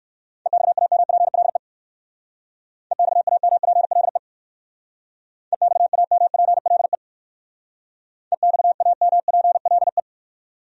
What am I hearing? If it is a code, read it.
Morse code, heard as a keyed tone, about 59 wpm, 700 Hz. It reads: EXAMPLE EXAMPLE EXAMPLE EXAMPLE